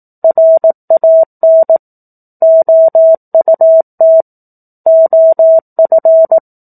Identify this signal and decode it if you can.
Morse code, heard as a keyed tone, about 18 wpm, 650 Hz. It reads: RAN OUT OF
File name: 04-Chapter-04 18 WPM